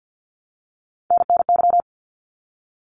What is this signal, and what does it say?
Morse code, heard as a keyed tone, about 48 wpm, 700 Hz. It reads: NNY
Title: ARRL Sections 50wpm